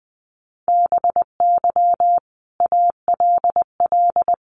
Morse code, heard as a keyed tone, about 20 wpm, 700 Hz. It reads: BY ALL
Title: Sentences from Top 1300 Words 20wpm